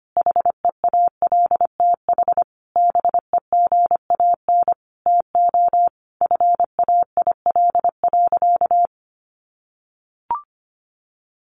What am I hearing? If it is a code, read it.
Morse code, heard as a keyed tone, about 25 wpm, 700 Hz. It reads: HEALTH BEGAN TO FAIL.
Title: Sentences from Top 1100 Words 25wpm